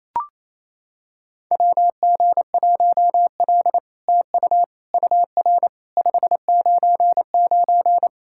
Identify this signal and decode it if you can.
Morse code, heard as a keyed tone, about 28 wpm, 700 Hz. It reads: WG1L TU UR 599
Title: SOTA QSO Practice 28wpm